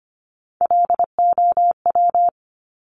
Morse code, heard as a keyed tone, about 25 wpm, 700 Hz. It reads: LOW
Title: Top Three Letter Combinations in Words 25wpm